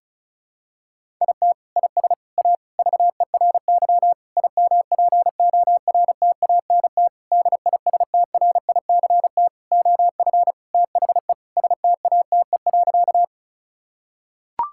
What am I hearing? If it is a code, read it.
Morse code, heard as a keyed tone, about 36 wpm, 700 Hz. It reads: IT IS A VERY IMPORTANT DISTRICT OF THE STATE.
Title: Sentences from Top 1600 Words 35wpm